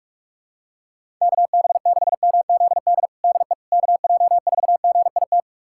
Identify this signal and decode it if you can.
Morse code, heard as a keyed tone, about 46 wpm, 700 Hz. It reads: KB6MZD DE KJ4GIT
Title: USA Callsign DE Callsign - 6 Characters - 45wpm